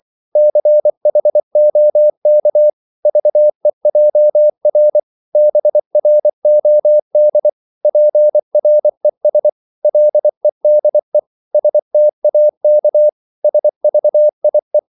Morse code, heard as a keyed tone, about 24 wpm, 600 Hz. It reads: CHOK VEJR BROD PRES LEDE STAK SVIE
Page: Korte danske ord 24wpm